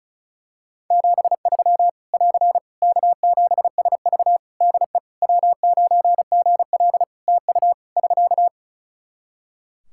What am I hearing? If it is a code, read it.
Morse code, heard as a keyed tone, about 34 wpm, 700 Hz. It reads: 73 <AR> K7SV DE W9GL TU <SK>
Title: Practice QSO - Exchange 4 - Ending - 35wpm